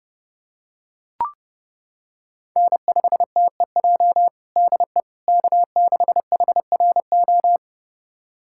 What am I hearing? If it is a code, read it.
Morse code, heard as a keyed tone, about 32 wpm, 700 Hz. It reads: N5TEJ DE K6HRO
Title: USA Callsign DE Callsign 30wpm